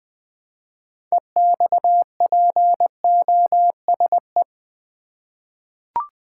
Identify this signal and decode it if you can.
Morse code, heard as a keyed tone, about 20 wpm, 700 Hz. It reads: EXPOSE
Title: 6 Letter Words - First 3 Letters Given - 20wpm